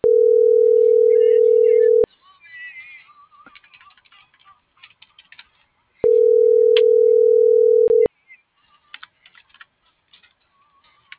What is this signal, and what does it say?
Ringback tone (North America), 440 + 480 Hz.
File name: CENTRAL MAYAB / out jorge.armando